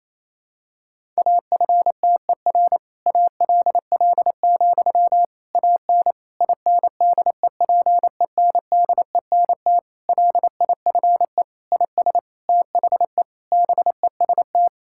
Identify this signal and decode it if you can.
Morse code, heard as a keyed tone, about 27 wpm, 700 Hz. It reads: AFTER ALL, AN INDEPENDENT LIFE IS THE BEST
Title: Sentences from Top 1000 Words - Review Part 6 - 28wpm